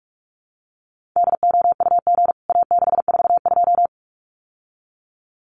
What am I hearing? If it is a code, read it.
Morse code, heard as a keyed tone, about 44 wpm, 700 Hz. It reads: DOUG A642
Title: CWT Exchange 45wpm